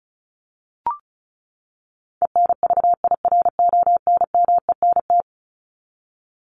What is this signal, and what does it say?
Morse code, heard as a keyed tone, about 34 wpm, 700 Hz. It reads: ENVIRONMENT
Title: Top 501-750 Words 35wpm